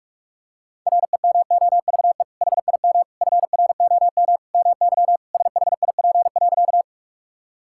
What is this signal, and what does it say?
Morse code, heard as a keyed tone, about 41 wpm, 700 Hz. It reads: REMOVE HIM FROM MY SHIP.
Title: Sentences from Top 1000 Words - Review Part 5 - 45wpm